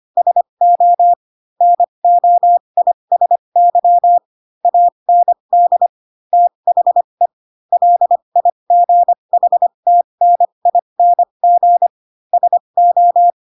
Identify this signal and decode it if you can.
Morse code, heard as a keyed tone, about 25 wpm, spaced as for 22 wpm, 700 Hz. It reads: SO NOISY AND THE LIGHTNING SO
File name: Pinocchio / Pinocchio ch7 18 25 ik1hge